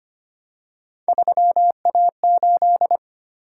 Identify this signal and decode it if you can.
Morse code, heard as a keyed tone, about 25 wpm, 700 Hz. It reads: 3A8
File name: Three Letter-Number Combinations - 25wpm